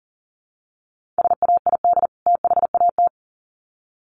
Morse code, heard as a keyed tone, about 40 wpm, 700 Hz. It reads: SAID THAT
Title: Sets of 2 Words 40wpm